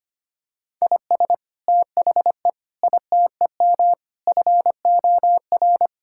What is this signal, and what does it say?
Morse code, heard as a keyed tone, about 25 wpm, 700 Hz. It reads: IS THE ITEM FOR
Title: Sentences from Top 1000 Words - Review Part 6 - 25wpm